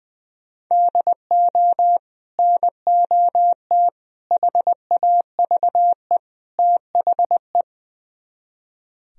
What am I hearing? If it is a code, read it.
Morse code, heard as a keyed tone, about 21 wpm, 700 Hz. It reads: DO NOT HAVE THE
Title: Speed-Racing - Sets of 4 Words 20wpm